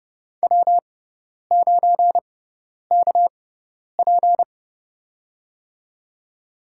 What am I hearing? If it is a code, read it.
Morse code, heard as a keyed tone, about 31 wpm, spaced as for 10 wpm, 700 Hz. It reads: W9KP
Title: USA Call Signs - 4 Characters - 6x Character Spacing - 30wpm